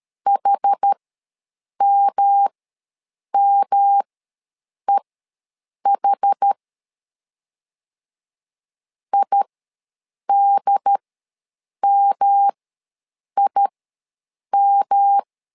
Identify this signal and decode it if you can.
Morse code, heard as a keyed tone, about 12 wpm, spaced as for 7 wpm, 800 Hz. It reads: HMMEH IDMIM
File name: LEZIONI / 3 MWRO / 3 M ws 12wpm 1745